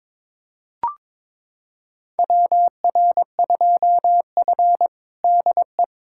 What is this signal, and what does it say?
Morse code, heard as a keyed tone, about 22 wpm, 700 Hz. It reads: WR2F DE
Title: USA Callsign DE Callsign - 4 Characters - 22wpm